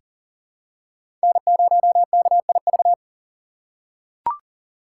Morse code, heard as a keyed tone, about 39 wpm, 700 Hz. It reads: N0KIV
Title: USA Call Signs - 5 Characters - 2x Character Spacing - 40wpm